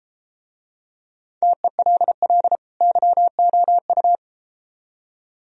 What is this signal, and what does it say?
Morse code, heard as a keyed tone, about 33 wpm, 700 Hz. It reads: TELL YOU